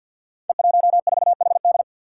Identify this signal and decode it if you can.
Morse code, heard as a keyed tone, about 53 wpm, 700 Hz. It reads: E14HD
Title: Five Challenging Letter-Number Combinations 50wpm